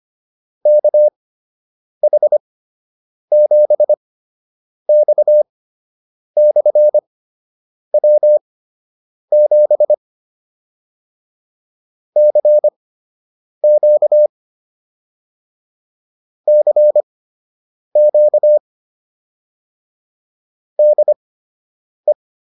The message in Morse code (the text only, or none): KH7X/W7 CQ CQ DE